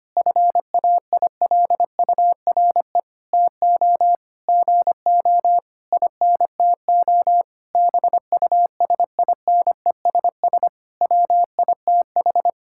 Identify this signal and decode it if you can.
Morse code, heard as a keyed tone, about 25 wpm, 700 Hz. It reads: FAILURE TO GO INTO BUSINESS WITH
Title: Sentences from Top 1000 Words - Review Part 6 - 25wpm